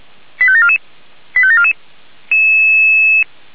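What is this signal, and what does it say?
Five-tone selective call (ZVEI), heard as quick runs of five tones, tones of about 70 ms. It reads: 85744 85744
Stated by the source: ZVEI five-tone selective call received by SDR.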